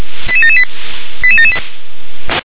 Five-tone selective call (ZVEI), heard as quick runs of five tones, tones of about 70 ms. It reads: 88778 88778